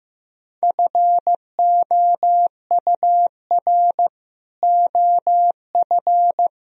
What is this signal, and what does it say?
Morse code, heard as a keyed tone, about 15 wpm, 700 Hz. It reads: FOUR OF
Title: Sentences from Top 1000 Words - Review Part 1 - 15wpm